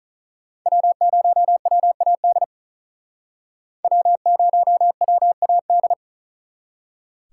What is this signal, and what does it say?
Morse code, heard as a keyed tone, about 45 wpm, 700 Hz. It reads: W0WAD W0WAD